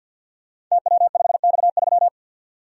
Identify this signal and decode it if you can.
Morse code, heard as a keyed tone, about 47 wpm, 700 Hz. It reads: TW5X3
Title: Code Groups - 5 Letter:Numbers - 50wpm